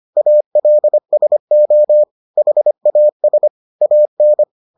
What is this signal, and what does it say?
Morse code, heard as a keyed tone, about 25 wpm, 600 Hz. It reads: ALSO HAS AN